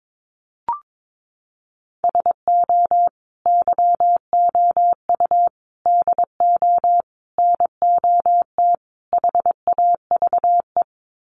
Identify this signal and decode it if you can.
Morse code, heard as a keyed tone, about 22 wpm, 700 Hz. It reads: SO YOU DO NOT HAVE
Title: Sets of 5 Words Encore 22wpm